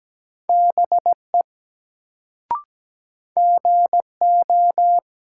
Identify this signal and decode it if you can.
Morse code, heard as a keyed tone, about 17 wpm, 700 Hz. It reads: BE GO
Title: Sets of 2 Words - New Word Drill - 17wpm